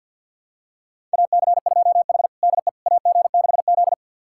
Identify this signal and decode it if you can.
Morse code, heard as a keyed tone, about 50 wpm, 700 Hz. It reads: AC2H DE AG6Z
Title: USA Callsign DE Callsign - 4 Characters - 50wpm